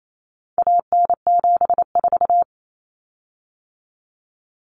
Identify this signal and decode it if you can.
Morse code, heard as a keyed tone, about 28 wpm, 700 Hz. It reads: AN74
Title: Common Cut Numbers - 4-Digit 28wpm